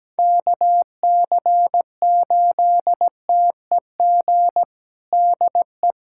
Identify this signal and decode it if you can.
Morse code, heard as a keyed tone, about 17 wpm, 700 Hz. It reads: KC8TEG DE
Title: USA Callsign DE Callsign 17wpm